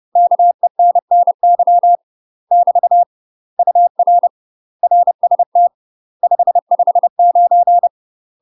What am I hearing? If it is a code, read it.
Morse code, heard as a keyed tone, about 30 wpm, 700 Hz. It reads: KENNY <BT> UR RST 559
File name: UK MASTER QSOs TRAINING/30WPM